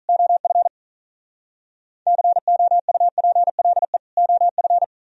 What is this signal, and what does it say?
Morse code, heard as a keyed tone, about 42 wpm, 700 Hz. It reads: OF COUPLE OF
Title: Speed-Racing - Sets of 2 Words 35wpm